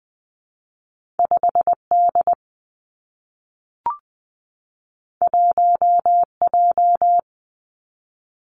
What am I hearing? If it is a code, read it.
Morse code, heard as a keyed tone, about 20 wpm, 700 Hz. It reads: 5 D 1 J
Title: Two Challenging Letter-Number Combinations 20wpm